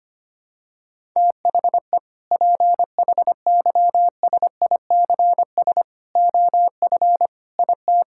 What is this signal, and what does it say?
Morse code, heard as a keyed tone, about 25 wpm, 700 Hz. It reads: THE PHYSICS OF IT
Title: Sentences from Top 1000 Words - Review Part 5 - 25wpm